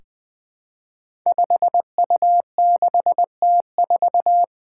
Morse code, heard as a keyed tone, about 20 wpm, 700 Hz. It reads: 5U6T4
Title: Five Challenging Letter-Number Combinations 20wpm